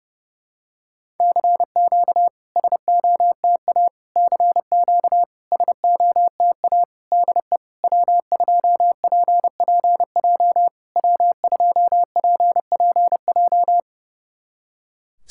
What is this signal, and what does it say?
Morse code, heard as a keyed tone, about 30 wpm, 700 Hz. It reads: CQ SOTA CQ SOTA DE W2PPJ W2PPJ
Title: SOTA QSO Practice 30wpm